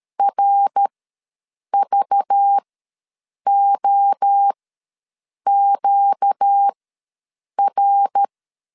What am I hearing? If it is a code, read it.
Morse code, heard as a keyed tone, about 13 wpm, spaced as for 7 wpm, 800 Hz. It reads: RVOQR
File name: LEZIONI / 5 LBVC / 5 LBV ws 12wpm 3015